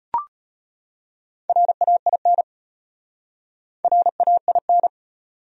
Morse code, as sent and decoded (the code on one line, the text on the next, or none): .-. .- .. -. / .-. .- .. -.
RAIN RAIN